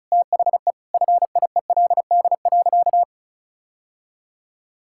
Morse code, heard as a keyed tone, about 34 wpm, 700 Hz. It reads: THE FIELD.
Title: Sentences from Top 1000 Words - Review Part 5 - 35wpm